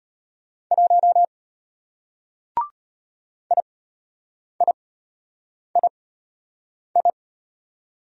Morse code, heard as a keyed tone, about 34 wpm, spaced as for 8 wpm, 700 Hz. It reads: 1 IIII